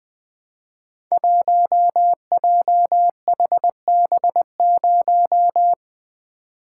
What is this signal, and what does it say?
Morse code, heard as a keyed tone, about 20 wpm, 700 Hz. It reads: 1JHB0
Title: Code Groups - 5 Letter:Numbers - 20wpm